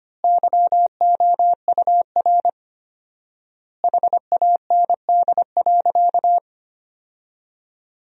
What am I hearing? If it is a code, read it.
Morse code, heard as a keyed tone, about 25 wpm, 700 Hz. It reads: YOUR HAND.
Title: 4x Word-Spacing - Sentences from Top 400 Words - 25wpm